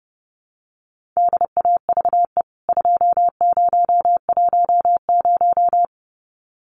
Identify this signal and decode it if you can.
Morse code, heard as a keyed tone, about 30 wpm, 700 Hz. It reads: DAVE 2010
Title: CWT Exchange 30wpm